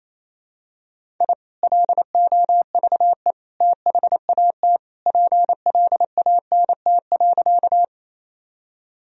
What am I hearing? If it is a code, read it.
Morse code, heard as a keyed tone, about 28 wpm, 700 Hz. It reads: I LOVE THAT PLANT.
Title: Sentences from Top 1000 Words - Review Part 1 - 28wpm